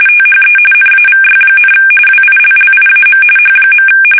RTTY, about 45.45 baud. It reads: THE NAVY, WILLIAM H.
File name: fsk850 / usnavy3